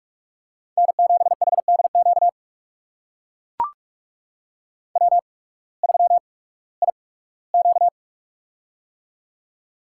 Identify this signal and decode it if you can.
Morse code, heard as a keyed tone, about 44 wpm, spaced as for 21 wpm, 700 Hz. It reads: N7HDQ W3IQ
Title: USA Call Signs - 8x Character Spacing - 45wpm